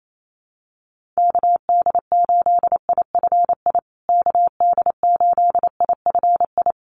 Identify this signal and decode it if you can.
Morse code, heard as a keyed tone, about 28 wpm, 700 Hz. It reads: KD8IFI KD8IFI
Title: USA Call Signs Repeated 2x 28wpm